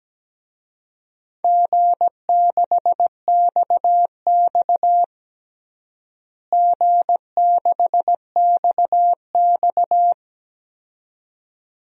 Morse code, heard as a keyed tone, about 17 wpm, 700 Hz. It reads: G6XX G6XX
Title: DXCC-100 Call Signs - 4 Characters - Repeated 2x - 17wpm